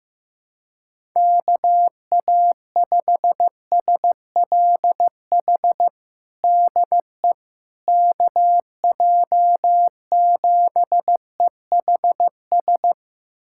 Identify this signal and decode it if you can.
Morse code, heard as a keyed tone, about 15 wpm, 700 Hz. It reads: KA5SLH DE KJ7EHS
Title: USA Callsign DE Callsign 15wpm